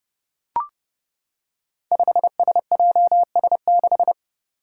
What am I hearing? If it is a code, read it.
Morse code, heard as a keyed tone, about 30 wpm, 700 Hz. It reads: 5SJS6
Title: Five Challenging Letter-Number Combinations 30wpm